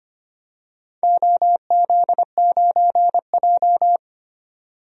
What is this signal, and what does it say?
Morse code, heard as a keyed tone, about 25 wpm, 700 Hz. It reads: OZ9J
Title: DXCC-100 Call Signs - 4 Characters - 25wpm